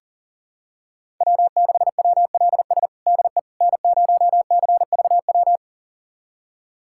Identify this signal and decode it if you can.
Morse code, heard as a keyed tone, about 40 wpm, 700 Hz. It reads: W6WLS DE N0CVW
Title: USA Callsign DE Callsign - 5 Characters - 40wpm